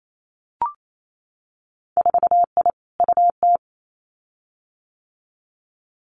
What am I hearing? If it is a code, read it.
Morse code, heard as a keyed tone, about 28 wpm, 700 Hz. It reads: 4I UT
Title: Winter Field Day Exchanges 28wpm